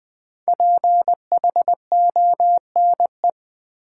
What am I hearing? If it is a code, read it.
Morse code, heard as a keyed tone, about 21 wpm, 700 Hz. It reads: PHONE
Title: Sentences from Top 1100 Words - New Word Drill - 20wpm